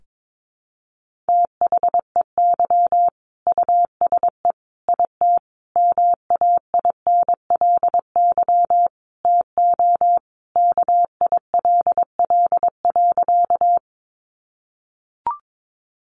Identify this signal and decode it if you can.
Morse code, heard as a keyed tone, about 22 wpm, 700 Hz. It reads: THEY USE IT MAINLY TO KILL.
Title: Sentences from Top 1400 Words 22wpm